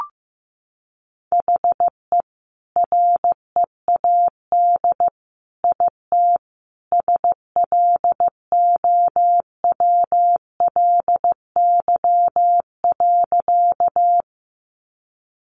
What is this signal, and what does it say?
Morse code, heard as a keyed tone, about 15 wpm, 700 Hz. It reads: HE READ IT SLOWLY.
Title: Sentences from Top 1100 Words 15wpm